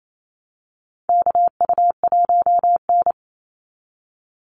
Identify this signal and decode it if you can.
Morse code, heard as a keyed tone, about 28 wpm, 700 Hz. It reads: KU1N
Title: Call Signs Repeated 2x - 28wpm